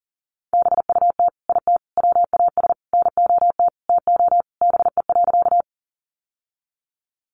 Morse code, heard as a keyed tone, about 40 wpm, 700 Hz. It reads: BUT IT WAS NOT TO BE.
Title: Sentences from Top 600 Words 40wpm